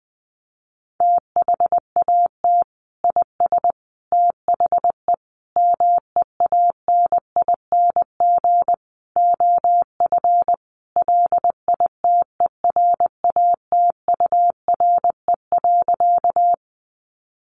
Morse code, heard as a keyed tone, about 20 wpm, 700 Hz. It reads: THAT IS THE MEANING OF LITERATURE.
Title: Sentences from Top 900 Words 20wpm